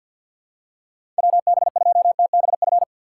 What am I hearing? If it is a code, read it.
Morse code, heard as a keyed tone, about 48 wpm, 700 Hz. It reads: WB2TBF
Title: USA Call Signs - 3x Character Spacing - 50wpm